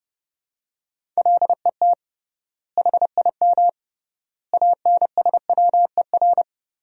Morse code, heard as a keyed tone, about 30 wpm, 700 Hz. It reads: LET HIM ANSWER
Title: 3x Word-Spacing - Sentences from Top 300 Words - 30wpm